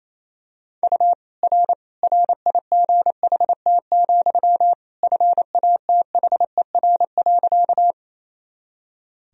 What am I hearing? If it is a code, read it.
Morse code, heard as a keyed tone, about 28 wpm, 700 Hz. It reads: U R RIGHT, FATHER.
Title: Sentences from Top 300 Words and Abbreviations 28wpm